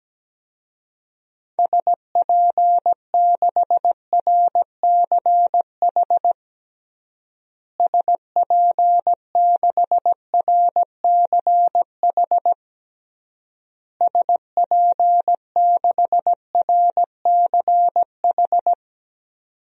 Morse code, heard as a keyed tone, about 17 wpm, 700 Hz. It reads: SP6RCH SP6RCH SP6RCH
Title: DXCC-100 Call Signs - 6 Characters - Repeated 3x - 17wpm